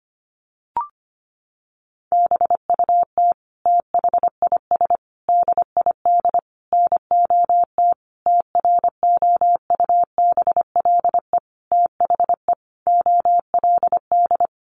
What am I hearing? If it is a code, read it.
Morse code, heard as a keyed tone, about 25 wpm, 700 Hz. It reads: BUT THIS DID NOT TROUBLE THE OLD
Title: Sentences from Top 1200 Words 25wpm